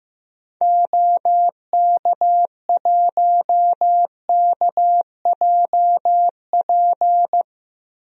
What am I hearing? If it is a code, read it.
Morse code, heard as a keyed tone, about 15 wpm, 700 Hz. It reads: OK1KJP
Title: DXCC-100 Call Signs - 6 Characters - 15wpm